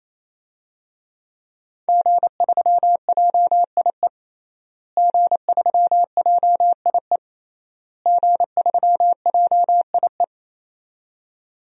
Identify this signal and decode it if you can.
Morse code, heard as a keyed tone, about 28 wpm, 700 Hz. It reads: G3JIE G3JIE G3JIE
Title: DXCC-100 Call Signs - 5 Characters - Repeated 3x - 28wpm